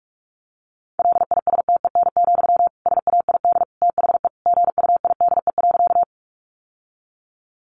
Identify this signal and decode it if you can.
Morse code, heard as a keyed tone, about 44 wpm, 700 Hz. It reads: LISTEN, SAID THE GUIDE.
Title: Sentences from Top 600 Words 45wpm